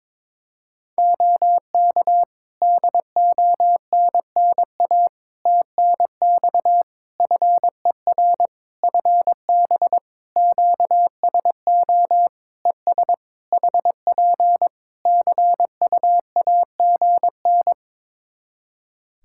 Morse code, heard as a keyed tone, about 22 wpm, 700 Hz. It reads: OK DONNA TNX FER FB QSO ES HP CUAGN
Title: Practice QSO - Exchange 4 - Ending - 22wpm